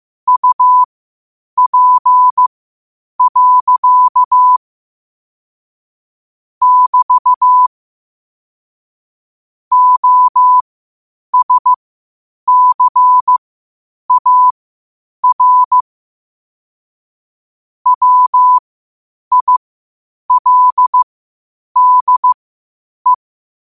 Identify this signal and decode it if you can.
Morse code, heard as a keyed tone, about 15 wpm, spaced as for 8 wpm, 1000 Hz. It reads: UP. <BT> OSCAR WILDE
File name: qotdfiles / 8wpm